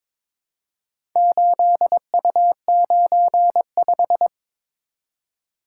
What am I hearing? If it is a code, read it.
Morse code, heard as a keyed tone, about 22 wpm, 700 Hz. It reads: 8U95
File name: practice / 22wpm / Four Letter-Number Combinations - 22wpm